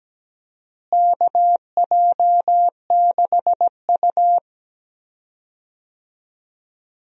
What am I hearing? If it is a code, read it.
Morse code, heard as a keyed tone, about 17 wpm, 700 Hz. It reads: KJ6U
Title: USA Call Signs - 4 Characters - Repeated 3x - 17wpm